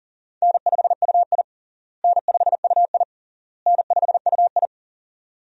Morse code, heard as a keyed tone, about 40 wpm, 700 Hz. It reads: N5UI N5UI N5UI
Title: USA Call Signs - 4 Characters - Repeated 3x - 40wpm